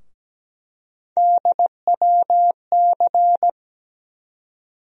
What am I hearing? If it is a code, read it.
Morse code, heard as a keyed tone, about 17 wpm, 700 Hz. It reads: DWC
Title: USA Call Sign Postfixes - 3 Letters 17wpm